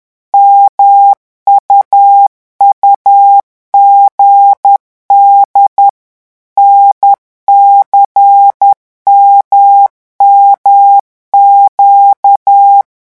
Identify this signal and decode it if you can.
Morse code, heard as a keyed tone, about 10 wpm, 800 Hz. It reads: MUUGD NCMMQ
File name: morse mp3 / revision3-vitesse 10 mots